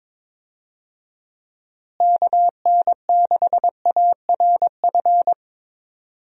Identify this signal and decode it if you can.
Morse code, heard as a keyed tone, about 22 wpm, 700 Hz. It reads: KN6ARF
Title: USA Call Signs - 6 Characters - Repeated 3x - 22wpm